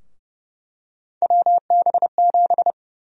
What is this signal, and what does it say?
Morse code, heard as a keyed tone, about 29 wpm, 700 Hz. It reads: WB7
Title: USA Call Sign Prefixes - Repeated 2x - 30wpm